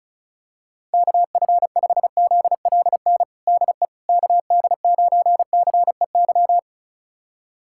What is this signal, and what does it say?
Morse code, heard as a keyed tone, about 34 wpm, 700 Hz. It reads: KF5ZLN DE KD9CEY
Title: USA Callsign DE Callsign 35wpm